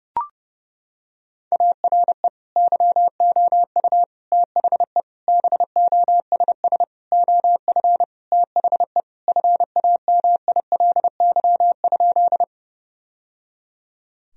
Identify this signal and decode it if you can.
Morse code, heard as a keyed tone, about 32 wpm, 700 Hz. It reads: ARE YOU THE BOSS OF THE FAMILY?
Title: Sentences from Top 1000 Words - Review Part 4 - 30wpm